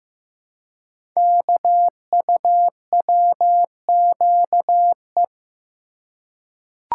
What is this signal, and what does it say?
Morse code, heard as a keyed tone, about 15 wpm, 700 Hz. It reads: KUWQE
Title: Code Groups - 5 Letters - 15wpm